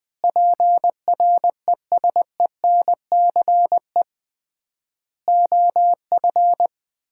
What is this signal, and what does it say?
Morse code, heard as a keyed tone, about 20 wpm, 700 Hz. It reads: PRESENCE OF
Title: 3x Word-Spacing - Sets of 3 Words - 20wpm